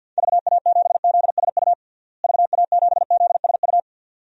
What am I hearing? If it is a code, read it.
Morse code, heard as a keyed tone, about 52 wpm, 700 Hz. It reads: VA7ZSU VA7ZSU
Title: DXCC-100 Call Signs - 6 Characters - Repeated 2x - 50wpm